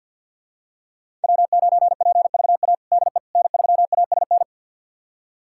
Morse code, heard as a keyed tone, about 52 wpm, 690 Hz. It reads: W9PVA DE N3ASN
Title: USA Callsign DE Callsign - 5 Characters - 50wpm